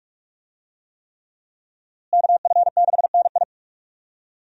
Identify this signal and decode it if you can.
Morse code, heard as a keyed tone, about 45 wpm, 700 Hz. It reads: KF6NI
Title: USA Call Signs - 5 Characters - Repeated 3x - 45wpm